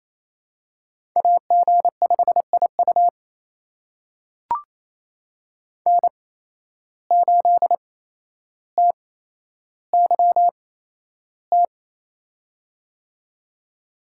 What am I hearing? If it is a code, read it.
Morse code, heard as a keyed tone, about 28 wpm, spaced as for 12 wpm, 700 Hz. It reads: AG5IU N8TYT
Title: USA Call Signs - 5 Characters - 8x Character Spacing - 28wpm